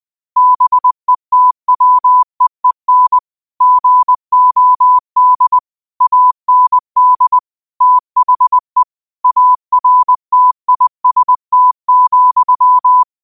Morse code, heard as a keyed tone, about 20 wpm, 1000 Hz. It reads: BETWEEN GOD AND THE ARTIST,